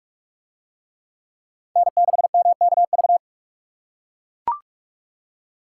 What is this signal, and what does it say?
Morse code, heard as a keyed tone, about 44 wpm, 700 Hz. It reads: N6MKV